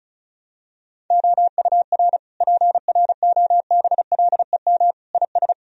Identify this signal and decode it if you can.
Morse code, heard as a keyed tone, about 36 wpm, 700 Hz. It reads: OUR PROBLEM IS
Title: Sentences from Top 1000 Words - Review Part 2 - 35wpm